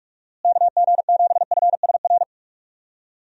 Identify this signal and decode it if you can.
Morse code, heard as a keyed tone, about 45 wpm, 700 Hz. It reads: KG7FSR
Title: USA Call Signs - 6 Characters - Repeated 2x - 45wpm